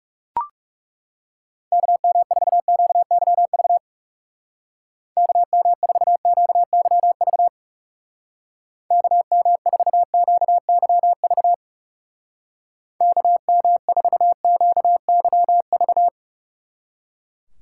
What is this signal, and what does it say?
Morse code, heard as a keyed tone, about 38 wpm, 700 Hz. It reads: KM4QYV KM4QYV KM4QYV KM4QYV